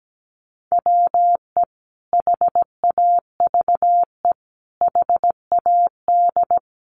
Morse code, heard as a keyed tone, about 17 wpm, 700 Hz. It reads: WE HAVE HAD
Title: Sentences from Top 1500 Words 17wpm